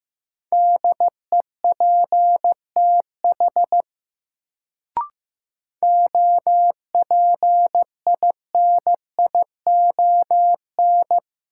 Morse code, heard as a keyed tone, about 15 wpm, 700 Hz. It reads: DEPTH OPINION
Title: Sentences from Top 1200 Words - New Word Drill - 15wpm